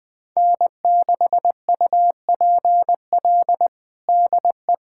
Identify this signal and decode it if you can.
Morse code, heard as a keyed tone, about 20 wpm, 700 Hz. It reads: N6UPL DE
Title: USA Callsign DE Callsign - 5 Characters - 20wpm